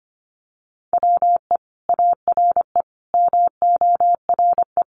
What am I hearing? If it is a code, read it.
Morse code, heard as a keyed tone, about 25 wpm, 700 Hz. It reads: WE ARE MORE
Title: Sentences from Top 500 Words 25wpm